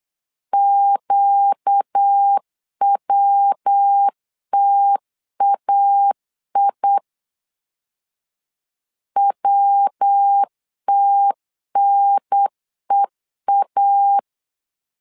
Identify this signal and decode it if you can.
Morse code, heard as a keyed tone, about 8 wpm, 800 Hz. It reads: QWTAI WTNEA
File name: 4 Q 8wpm